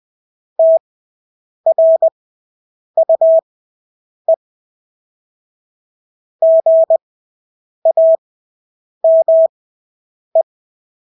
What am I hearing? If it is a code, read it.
Morse code, heard as a keyed tone, about 20 wpm, spaced as for 8 wpm, 650 Hz. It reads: TRUE GAME
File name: Engelske ord 20/8 wpm